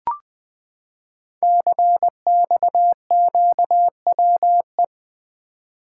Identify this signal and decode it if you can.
Morse code, heard as a keyed tone, about 20 wpm, 700 Hz. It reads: CXQWE
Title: Code Groups - 5 Letters - 20wpm